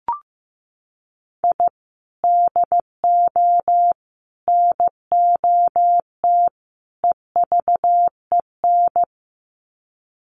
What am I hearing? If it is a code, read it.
Morse code, heard as a keyed tone, about 15 wpm, 700 Hz. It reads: I DO NOT EVEN
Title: Sets of 4 Words 15wpm